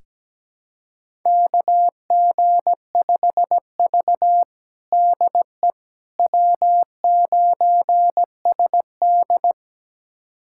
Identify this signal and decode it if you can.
Morse code, heard as a keyed tone, about 17 wpm, 700 Hz. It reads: KG5V DE W9SD
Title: USA Callsign DE Callsign - 4 Characters - 17wpm